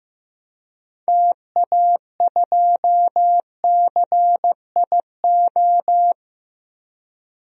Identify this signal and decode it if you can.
Morse code, heard as a keyed tone, about 15 wpm, 700 Hz. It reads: TA2CIO
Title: DXCC-100 Call Signs - 6 Characters - 15wpm